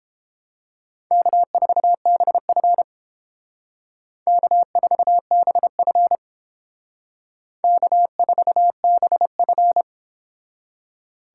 Morse code, heard as a keyed tone, about 29 wpm, 700 Hz. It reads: K4BF K4BF K4BF